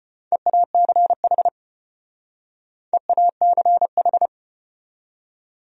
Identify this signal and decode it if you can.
Morse code, heard as a keyed tone, about 33 wpm, 700 Hz. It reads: EACH EACH